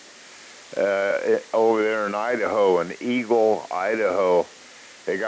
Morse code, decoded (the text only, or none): none